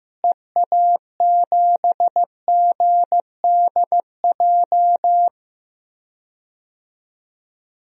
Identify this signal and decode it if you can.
Morse code, heard as a keyed tone, about 15 wpm, 700 Hz. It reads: EA7GDJ
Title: DXCC-100 Call Signs - 6 Characters - Repeated 3x - 15wpm